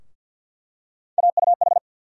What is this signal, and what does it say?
Morse code, heard as a keyed tone, about 47 wpm, 700 Hz. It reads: AUH
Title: USA Call Sign Postfixes - 3 Letters 50wpm